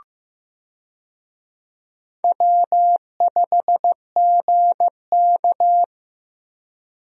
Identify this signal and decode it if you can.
Morse code, heard as a keyed tone, about 15 wpm, 700 Hz. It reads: W5GK
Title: USA Call Signs - 4 Characters - Repeated 3x - 15wpm